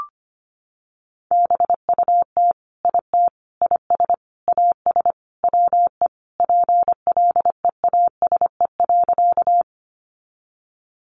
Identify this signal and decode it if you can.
Morse code, heard as a keyed tone, about 25 wpm, 700 Hz. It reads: BUT IT IS AS WE PLEASE.
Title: Sentences from Top 1500 Words 25wpm